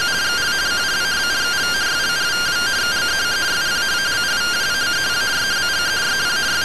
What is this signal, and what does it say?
Two-tone teleprinter signal, about 170 Hz shift.